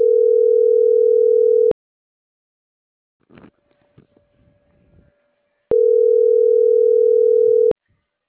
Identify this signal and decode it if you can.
Ringback tone (North America), 440 + 480 Hz.